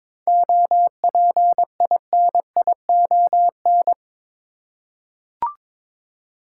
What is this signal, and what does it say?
Morse code, heard as a keyed tone, about 22 wpm, 700 Hz. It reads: OPINION
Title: Seven Letter Words - 22wpm